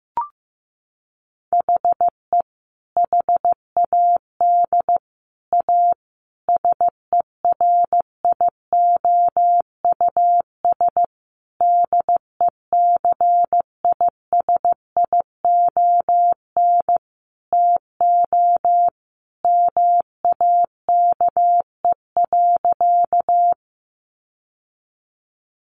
Morse code, heard as a keyed tone, about 15 wpm, 700 Hz. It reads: HE HAD A SERIOUS DECISION TO MAKE.
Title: Sentences from Top 1100 Words 15wpm